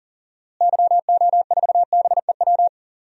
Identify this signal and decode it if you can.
Morse code, heard as a keyed tone, about 37 wpm, 700 Hz. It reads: YO4BEW
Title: DXCC-100 Call Signs - 6 Characters - Repeated 3x - 40wpm